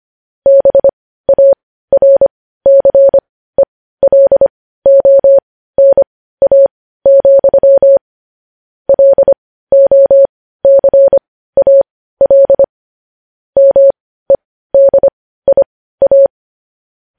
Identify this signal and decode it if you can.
Morse code, heard as a keyed tone, about 25 wpm, spaced as for 15 wpm, 550 Hz. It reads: BARCELONA, LOCAL MEDIA